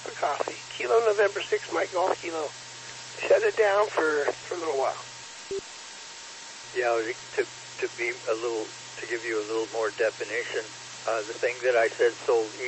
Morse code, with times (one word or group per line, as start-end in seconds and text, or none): none